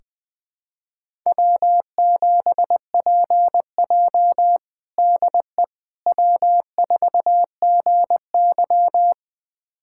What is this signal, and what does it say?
Morse code, heard as a keyed tone, about 20 wpm, 700 Hz. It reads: W7PJ DE W4GY
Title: USA Callsign DE Callsign - 4 Characters - 20wpm